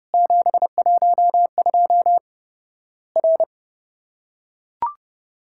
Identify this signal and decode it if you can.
Morse code, heard as a keyed tone, about 29 wpm, 700 Hz. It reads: 712 R
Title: Worked All Europe DX Contest - QTC Exchange - 30wpm